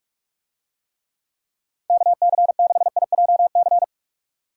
Morse code, heard as a keyed tone, about 44 wpm, 700 Hz. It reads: KC6IJC